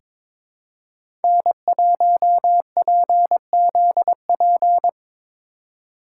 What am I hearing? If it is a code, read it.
Morse code, heard as a keyed tone, about 22 wpm, 700 Hz. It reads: N1PZP
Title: USA Call Signs - 5 Characters - 6x Character Spacing - 22wpm